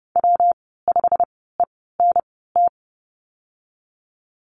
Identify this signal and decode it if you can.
Morse code, heard as a keyed tone, about 30 wpm, spaced as for 17 wpm, 700 Hz. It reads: W5ENT
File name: Lesson 19g - Letter w - Call Signs 30wpm